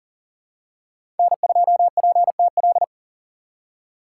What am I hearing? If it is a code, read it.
Morse code, heard as a keyed tone, about 40 wpm, 700 Hz. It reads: N2PTL
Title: USA Call Signs - 5 Characters - 4x Character Spacing - 40wpm